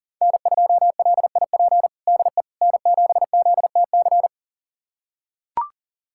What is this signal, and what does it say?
Morse code, heard as a keyed tone, about 37 wpm, 700 Hz. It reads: N2LIP DE N7ZTC
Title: USA Callsign DE Callsign - 5 Characters - 40wpm